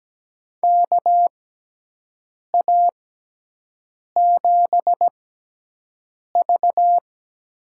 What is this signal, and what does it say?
Morse code, heard as a keyed tone, about 17 wpm, spaced as for 6 wpm, 700 Hz. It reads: KA7V
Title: USA Call Signs - 6 Characters - 6x Character Spacing - 17wpm